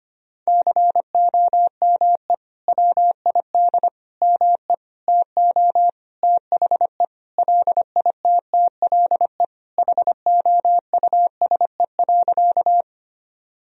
Morse code, heard as a keyed tone, about 25 wpm, 700 Hz. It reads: COME WID ME TO THE LITTLE HOUSE.
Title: Sentences from Top 200 Words and Abbreviations 25wpm